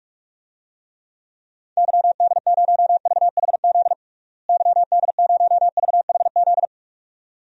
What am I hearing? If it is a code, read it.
Morse code, heard as a keyed tone, about 46 wpm, 700 Hz. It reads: YD0VHZ YD0VHZ
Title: DXCC-100 Call Signs - 6 Characters - Repeated 2x - 45wpm